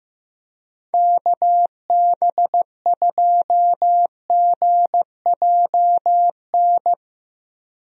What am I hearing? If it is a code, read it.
Morse code, heard as a keyed tone, about 15 wpm, 700 Hz. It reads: KB2GJN
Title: USA Call Signs - 6 Characters - 15wpm